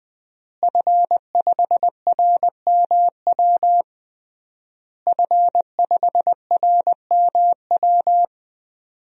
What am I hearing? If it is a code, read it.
Morse code, heard as a keyed tone, about 21 wpm, 700 Hz. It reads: F5RMW F5RMW
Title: DXCC-100 Call Signs - 5 Characters - Repeated 3x - 20wpm